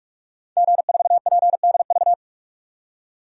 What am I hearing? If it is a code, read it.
Morse code, heard as a keyed tone, about 46 wpm, 700 Hz. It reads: G4PDV
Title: DXCC-100 Call Signs - 5 Characters - 45wpm